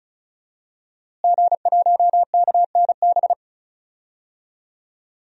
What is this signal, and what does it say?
Morse code, heard as a keyed tone, about 36 wpm, 700 Hz. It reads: G1KNB
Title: DXCC-100 Call Signs - 5 Characters - Repeated 2x - 35wpm